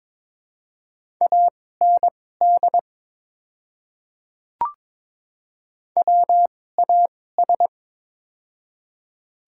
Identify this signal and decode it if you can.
Morse code, heard as a keyed tone, about 22 wpm, spaced as for 16 wpm, 700 Hz. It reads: AND WAS
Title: 2x Character Spacing - Top 100 Words - 22wpm